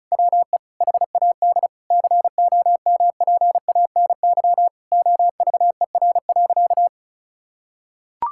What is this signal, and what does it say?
Morse code, heard as a keyed tone, about 34 wpm, 700 Hz. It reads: WE HAD COMPANY OVER.
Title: Sentences from Top 1000 Words - Review Part 6 - 35wpm